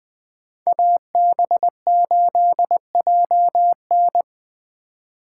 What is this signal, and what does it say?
Morse code, heard as a keyed tone, about 21 wpm, 700 Hz. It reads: AB8JN